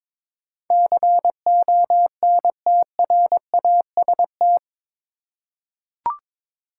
Morse code, heard as a keyed tone, about 22 wpm, 700 Hz. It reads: CONTRAST
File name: practice / 22wpm / Eight Letter Words - 22wpm